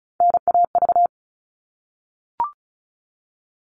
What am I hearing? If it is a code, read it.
Morse code, heard as a keyed tone, about 34 wpm, 700 Hz. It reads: NAV
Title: All Cut Numbers - 3-Digit 35wpm